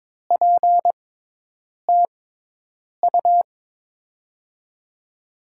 Morse code, heard as a keyed tone, about 22 wpm, spaced as for 8 wpm, 700 Hz. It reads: PTU
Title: USA Call Signs - 6x Character Spacing - 22wpm